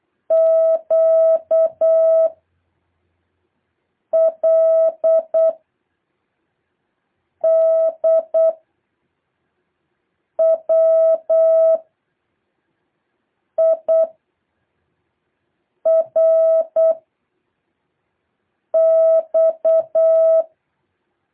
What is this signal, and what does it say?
Morse code, heard as a keyed tone, about 8 wpm, 650 Hz. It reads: Q L D W I R X
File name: jh8cbh / 2023cw